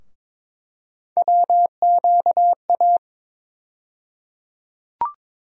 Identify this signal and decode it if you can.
Morse code, heard as a keyed tone, about 22 wpm, 700 Hz. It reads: WQA
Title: USA Call Sign Postfixes - 3 Letters 22wpm